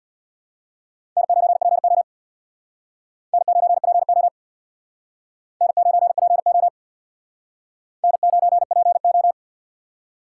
Morse code, heard as a keyed tone, about 64 wpm, 690 Hz. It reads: N9PO N9PO N9PO N9PO